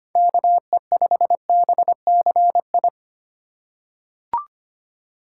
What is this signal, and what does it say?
Morse code, heard as a keyed tone, about 25 wpm, 700 Hz. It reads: KE5BCI
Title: USA Call Signs - 4x Character Spacing - 25wpm